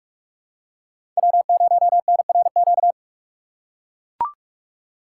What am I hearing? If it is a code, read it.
Morse code, heard as a keyed tone, about 41 wpm, 700 Hz. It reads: W0NRQ
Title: USA Call Signs - 6x Character Spacing - 45wpm